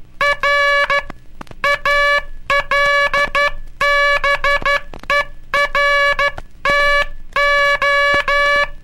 Morse code, heard as a keyed tone, about 12 wpm, 1700 Hz. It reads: R ALBERTO
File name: cursos / Curso CW mp3 / CW Aula - 18